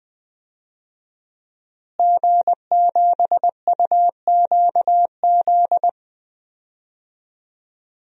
Morse code, heard as a keyed tone, about 20 wpm, 700 Hz. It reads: G7UQZ
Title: DXCC-100 Call Signs - 5 Characters - Repeated 2x - 20wpm